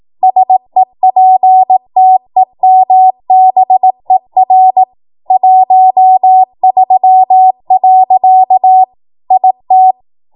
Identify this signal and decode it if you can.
Morse code, heard as a keyed tone, about 18 wpm, 750 Hz. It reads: SEPTEMBER 13. IT
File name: Morse / 18WPM